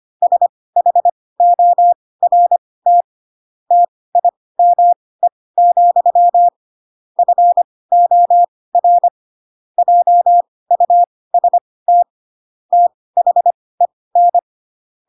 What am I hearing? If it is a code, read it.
Morse code, heard as a keyed tone, about 25 wpm, spaced as for 18 wpm, 700 Hz. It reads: SHORT TIME, FOR JUST THEN
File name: Pinocchio / Pinocchio ch4 12 25 ik1hge